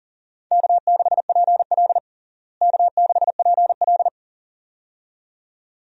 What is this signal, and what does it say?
Morse code, heard as a keyed tone, about 40 wpm, 700 Hz. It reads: K6PL K6PL
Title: USA Call Signs - 4 Characters - Repeated 2x - 40wpm